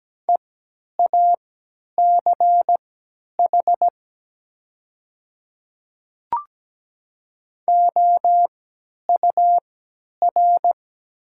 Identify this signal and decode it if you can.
Morse code, heard as a keyed tone, about 17 wpm, spaced as for 10 wpm, 700 Hz. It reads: EACH OUR
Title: 3x Character Spacing - Top 200 Words - 17wpm